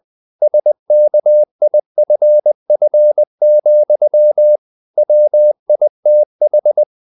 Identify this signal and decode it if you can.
Morse code, heard as a keyed tone, about 20 wpm, 600 Hz. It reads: SKIFF, WITH
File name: the old man and the sea part 2 20wpm